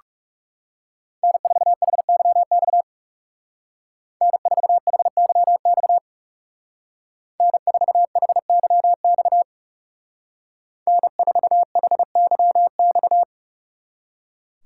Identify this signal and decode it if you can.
Morse code, heard as a keyed tone, about 39 wpm, 700 Hz. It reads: N4HYX N4HYX N4HYX N4HYX